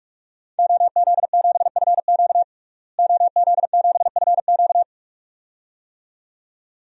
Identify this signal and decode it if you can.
Morse code, heard as a keyed tone, about 43 wpm, 700 Hz. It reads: OZ7FQ OZ7FQ
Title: DXCC-100 Call Signs - 5 Characters - Repeated 2x - 45wpm